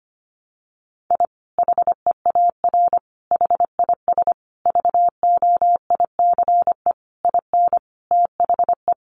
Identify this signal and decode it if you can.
Morse code, heard as a keyed tone, about 25 wpm, 700 Hz. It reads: I HEAR HIS VOICE IN THE
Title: Sentences from Top 500 Words 25wpm